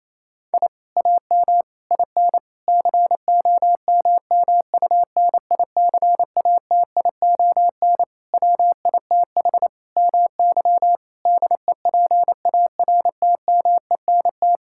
Morse code, heard as a keyed tone, about 28 wpm, 700 Hz. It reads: I AM IN COMMUNICATION WITH MY DEPARTMENT
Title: Sentences from Top 1000 Words - Review Part 6 - 28wpm